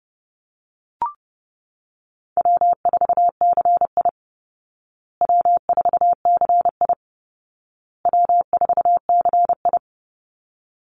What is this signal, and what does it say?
Morse code, heard as a keyed tone, about 30 wpm, 700 Hz. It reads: W4CI W4CI W4CI
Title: Easy Call Signs Repeated 3x - 30wpm